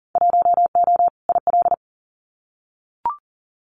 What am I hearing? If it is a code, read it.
Morse code, heard as a keyed tone, about 40 wpm, 700 Hz. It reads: 1O IL
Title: Winter Field Day Exchanges 40wpm